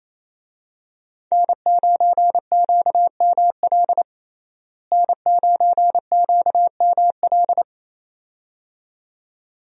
Morse code, heard as a keyed tone, about 28 wpm, 700 Hz. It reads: N9QML N9QML
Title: USA Call Signs - 5 Characters - Repeated 2x - 28wpm